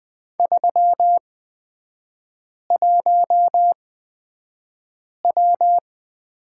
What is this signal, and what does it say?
Morse code, heard as a keyed tone, about 20 wpm, spaced as for 5 wpm, 700 Hz. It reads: 31W
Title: 243 Lesson 39e - Letter z - Three Characters 20wpm